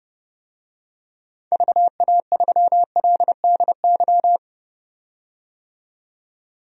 Morse code, heard as a keyed tone, about 32 wpm, 700 Hz. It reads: VA3LDY
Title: DXCC-100 Call Signs - 6 Characters - 30wpm